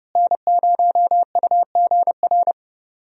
Morse code, heard as a keyed tone, about 30 wpm, 700 Hz. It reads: N0UGR
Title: USA Call Signs - 5 Characters - Repeated 3x - 30wpm